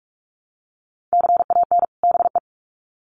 Morse code, heard as a keyed tone, about 44 wpm, 700 Hz. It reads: CAN BE
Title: Sets of 2 Words 45wpm